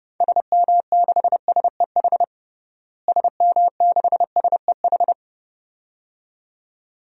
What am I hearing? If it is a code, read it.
Morse code, heard as a keyed tone, about 32 wpm, 700 Hz. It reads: SM6SEH SM6SEH
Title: DXCC-100 Call Signs - 6 Characters - Repeated 2x - 30wpm